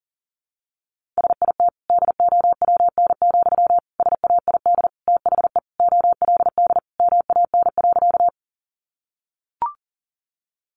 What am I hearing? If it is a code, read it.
Morse code, heard as a keyed tone, about 40 wpm, 700 Hz. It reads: SIT DOWN, SAID THE OLD MAN.
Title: Sentences from Top 500 Words 40wpm